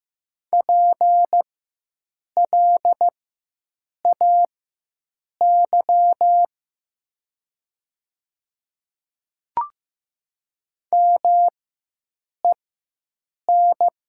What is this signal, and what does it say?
Morse code, heard as a keyed tone, about 15 wpm, spaced as for 7 wpm, 700 Hz. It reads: PLAY MEN
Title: 4x Character Spacing - Top 200 Words - 15wpm